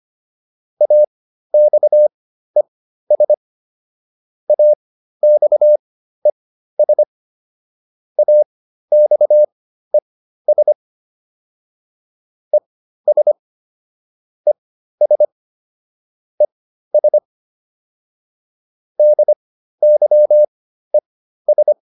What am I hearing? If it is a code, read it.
Morse code, heard as a keyed tone, about 25 wpm, spaced as for 13 wpm, 600 Hz. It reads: AXES AXES AXES ES ES ES DYES